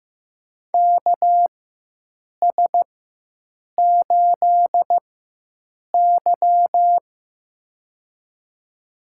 Morse code, heard as a keyed tone, about 15 wpm, spaced as for 7 wpm, 700 Hz. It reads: KS8Y
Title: USA Call Signs - 4 Characters - 4x Character Spacing - 15wpm